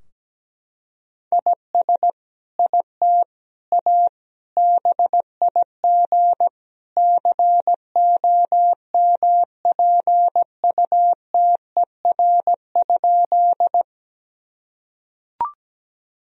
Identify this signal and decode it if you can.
Morse code, heard as a keyed tone, about 17 wpm, 700 Hz. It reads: IS IT A BIG COMPUTER?
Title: Sentences from Top 1000 Words - Review Part 3 - 17wpm